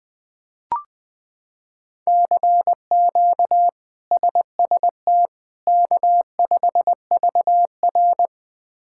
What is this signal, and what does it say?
Morse code, heard as a keyed tone, about 20 wpm, 700 Hz. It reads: CQ SST K5VR
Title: K1USN SST QSO Practice - 20wpm